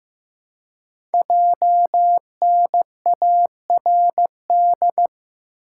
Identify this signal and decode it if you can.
Morse code, heard as a keyed tone, about 15 wpm, 700 Hz. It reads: JNARD
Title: Code Groups - 5 Letters - 15wpm